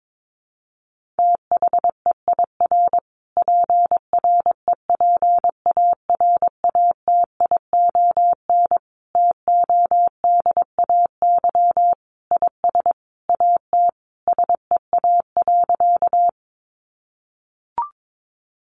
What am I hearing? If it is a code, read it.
Morse code, heard as a keyed tone, about 22 wpm, 700 Hz. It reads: THEIR PREPARATION TODAY IS AT SEA.
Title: Sentences from Top 1400 Words 22wpm